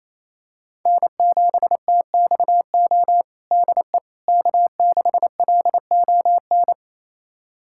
Morse code, heard as a keyed tone, about 28 wpm, 700 Hz. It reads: N7TXO DE K6LON
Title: USA Callsign DE Callsign - 5 Characters - 28wpm